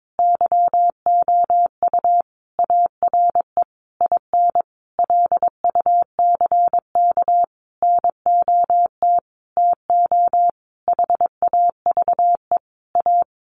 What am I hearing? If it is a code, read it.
Morse code, heard as a keyed tone, about 22 wpm, 700 Hz. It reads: YOU ARE IN LUCK NOT TO HAVE A
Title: Sentences from Top 1500 Words 22wpm